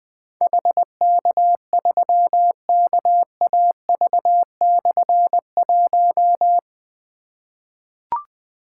Morse code, heard as a keyed tone, about 20 wpm, 700 Hz. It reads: HK3KAV/1
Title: Difficult Call Signs 20wpm